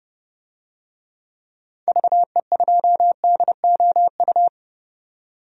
Morse code, heard as a keyed tone, about 32 wpm, 700 Hz. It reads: VE2DOU
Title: DXCC-100 Call Signs - 6 Characters - 30wpm